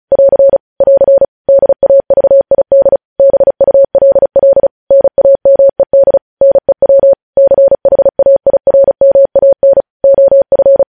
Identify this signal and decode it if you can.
Morse code, heard as a keyed tone, about 36 wpm, 550 Hz. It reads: <AR> <AR> DAVID BULL NAMED NEW CHAIRMAN OF
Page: news headlines in morse code at 35 wpm